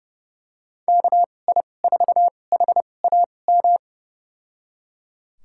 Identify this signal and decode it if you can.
Morse code, heard as a keyed tone, about 30 wpm, spaced as for 22 wpm, 700 Hz. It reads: KI4HAM
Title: USA Call Signs - 6 Characters - 2x Character Spacing - 30wpm